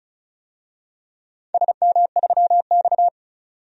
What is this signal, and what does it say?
Morse code, heard as a keyed tone, about 35 wpm, 700 Hz. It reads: SM3X
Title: DXCC-100 Call Signs - 4 Characters - 35wpm